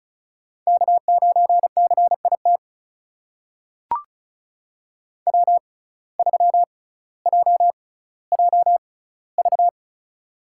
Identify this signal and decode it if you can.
Morse code, heard as a keyed tone, about 36 wpm, spaced as for 18 wpm, 700 Hz. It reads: K9CIT W3JJV
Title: USA Call Signs - 5 Characters - 6x Character Spacing - 35wpm